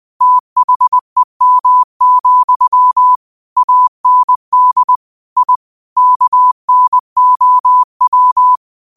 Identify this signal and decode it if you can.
Morse code, heard as a keyed tone, about 20 wpm, 1000 Hz. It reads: THEM, AND I KNOW
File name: qotdfiles / 20wpm